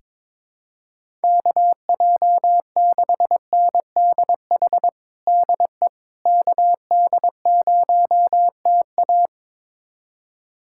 Morse code, heard as a keyed tone, about 22 wpm, 700 Hz. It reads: KJ6NDH DE KD0TA
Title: USA Callsign DE Callsign 22wpm